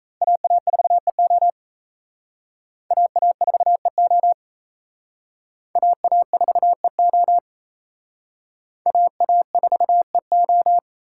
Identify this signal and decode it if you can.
Morse code, heard as a keyed tone, about 34 wpm, 700 Hz. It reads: AA4EO AA4EO AA4EO AA4EO
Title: Speed Racing - USA 2x2 Call Signs 28wpm